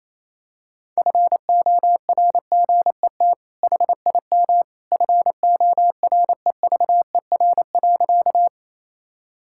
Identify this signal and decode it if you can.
Morse code, heard as a keyed tone, about 27 wpm, 700 Hz. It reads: FORGET HIM FOREVER.
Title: Sentences from Top 1600 Words 28wpm